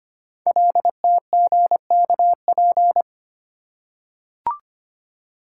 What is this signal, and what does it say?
Morse code, heard as a keyed tone, about 25 wpm, 700 Hz. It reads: LTGKP
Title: Code Groups - 5 Letter:Numbers - 25wpm